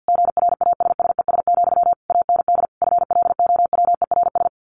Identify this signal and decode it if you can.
Morse code, heard as a keyed tone, about 50 wpm, 700 Hz. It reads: GRASSES, AND FLOWERS
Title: A Sound Of Thunder 50wpm